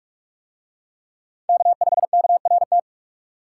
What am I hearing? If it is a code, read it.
Morse code, heard as a keyed tone, about 44 wpm, 700 Hz. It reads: K5KRT